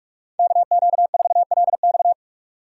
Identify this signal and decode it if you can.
Morse code, heard as a keyed tone, about 45 wpm, 700 Hz. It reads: KQ4LX
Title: USA Call Signs - 4x Character Spacing - 45wpm